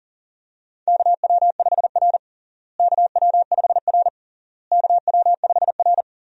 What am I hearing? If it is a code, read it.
Morse code, heard as a keyed tone, about 40 wpm, 700 Hz. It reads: KW5R KW5R KW5R
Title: USA Call Signs - 4 Characters - Repeated 3x - 40wpm